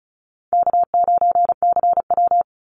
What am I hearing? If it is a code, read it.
Morse code, heard as a keyed tone, about 34 wpm, 700 Hz. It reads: K9CW
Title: Easy Call Signs Repeated 2x - 35wpm